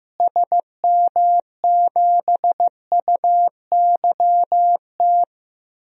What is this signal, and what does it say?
Morse code, heard as a keyed tone, about 15 wpm, 700 Hz. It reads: SM7UYT
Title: DXCC-100 Call Signs - 6 Characters - Repeated 2x - 15wpm